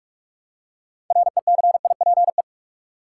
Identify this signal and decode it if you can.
Morse code, heard as a keyed tone, about 46 wpm, 700 Hz. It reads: RECIPE
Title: Six Letter Words - 45wpm